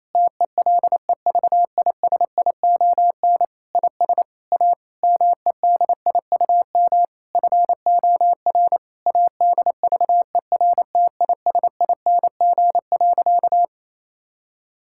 Morse code, heard as a keyed tone, about 28 wpm, 700 Hz. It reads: TELEVISION IS A MEDIUM FOR ADVERTISING.
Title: Sentences from Top 1000 Words - Review Part 6 - 28wpm